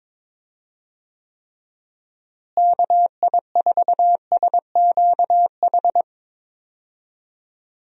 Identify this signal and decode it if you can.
Morse code, heard as a keyed tone, about 22 wpm, 700 Hz. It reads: KI4SQH
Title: USA Call Signs - 6 Characters - Repeated 3x - 22wpm